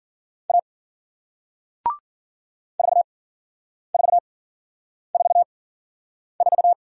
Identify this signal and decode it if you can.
Morse code, heard as a keyed tone, about 48 wpm, spaced as for 9 wpm, 700 Hz. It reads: A 4444
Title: Speed-Racing Single Letter-Number 40wpm